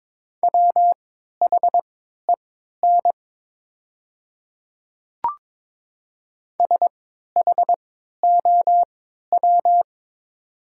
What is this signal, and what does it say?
Morse code, heard as a keyed tone, about 23 wpm, spaced as for 13 wpm, 700 Hz. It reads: WHEN SHOW
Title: 3x Character Spacing - Top 500 Words - 22wpm